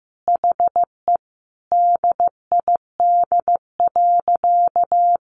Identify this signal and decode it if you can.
Morse code, heard as a keyed tone, about 14 wpm, 700 Hz. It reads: HE DID.
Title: Sentences from Top 1200 Words 15wpm